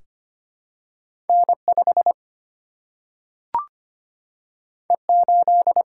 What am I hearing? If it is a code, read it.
Morse code, heard as a keyed tone, about 25 wpm, 700 Hz. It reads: N5 E8
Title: Single-Letter Single-Number 25wpm